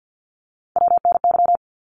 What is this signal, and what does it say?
Morse code, heard as a keyed tone, about 50 wpm, 700 Hz. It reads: WNY
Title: ARRL Sections 50wpm